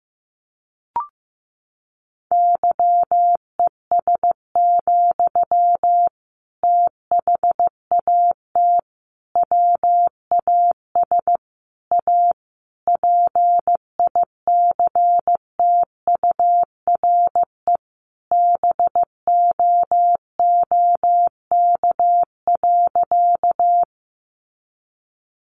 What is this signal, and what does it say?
Morse code, heard as a keyed tone, about 15 wpm, 700 Hz. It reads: YES, THAT WAS A PICTURE BOOK.
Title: Sentences from Top 500 Words 15wpm